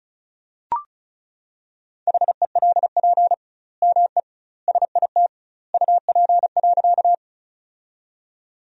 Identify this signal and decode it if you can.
Morse code, heard as a keyed tone, about 36 wpm, 700 Hz. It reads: HELP ME SIT UP.
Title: 2x Word-Spacing - Sentences from Top 500 Words - 35wpm